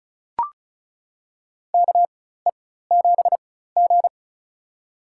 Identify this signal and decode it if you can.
Morse code, heard as a keyed tone, about 36 wpm, spaced as for 17 wpm, 700 Hz. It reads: KE7G
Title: USA Call Signs - 4 Characters - 4x Character Spacing - 35wpm